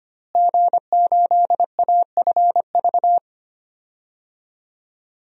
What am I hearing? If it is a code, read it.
Morse code, heard as a keyed tone, about 25 wpm, 700 Hz. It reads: G8AFV
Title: DXCC-100 Call Signs - 5 Characters - Repeated 2x - 25wpm